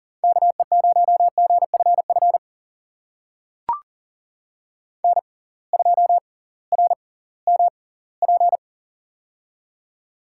Morse code, heard as a keyed tone, about 40 wpm, spaced as for 22 wpm, 700 Hz. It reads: KE0GFF N2RMP
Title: USA Call Signs - 6x Character Spacing - 40wpm